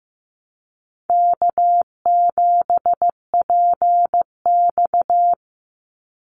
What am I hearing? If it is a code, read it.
Morse code, heard as a keyed tone, about 15 wpm, 700 Hz. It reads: K7PX
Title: NA SOTA Activator Callsigns 15wpm